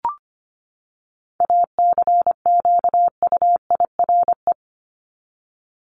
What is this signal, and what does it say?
Morse code, heard as a keyed tone, about 25 wpm, 700 Hz. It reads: ACQUIRE
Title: Top 1501-1750 Words 25wpm